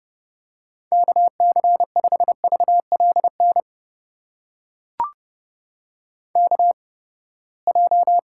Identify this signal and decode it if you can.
Morse code, heard as a keyed tone, about 30 wpm, spaced as for 21 wpm, 700 Hz. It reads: KC5VLN KJ
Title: USA Call Signs - 6 Characters - 8x Character Spacing - 30wpm